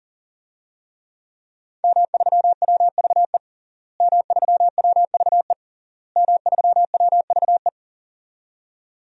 Morse code, heard as a keyed tone, about 40 wpm, 700 Hz. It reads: M3WVE M3WVE M3WVE
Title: DXCC-100 Call Signs - 5 Characters - Repeated 3x - 40wpm